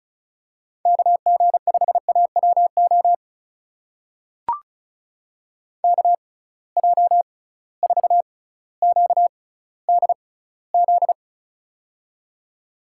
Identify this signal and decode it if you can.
Morse code, heard as a keyed tone, about 36 wpm, spaced as for 18 wpm, 700 Hz. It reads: KG5AWO KJ4QDZ
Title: USA Call Signs - 6 Characters - 6x Character Spacing - 35wpm